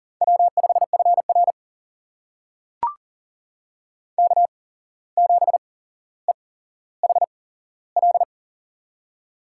Morse code, heard as a keyed tone, about 40 wpm, spaced as for 16 wpm, 690 Hz. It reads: W5FR K7EHL
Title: USA Call Signs - 8x Character Spacing - 40wpm